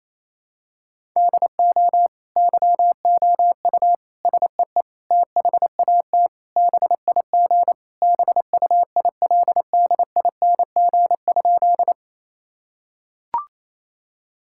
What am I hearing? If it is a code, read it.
Morse code, heard as a keyed tone, about 28 wpm, 700 Hz. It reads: DO YOU SEE THAT BIG BUILDING?
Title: Sentences from Top 1000 Words - Review Part 5 - 28wpm